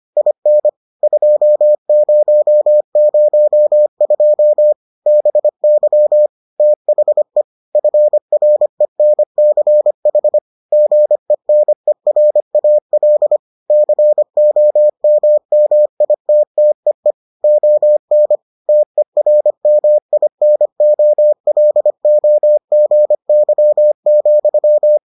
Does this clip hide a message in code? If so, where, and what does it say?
Yes: Morse code from 0.2 to 25.0 s, heard as a keyed tone, about 25 wpm, 600 Hz. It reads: IN 2002 BY THE FRENCH GENERAL COMMITTEE ON TERMINOLOGY,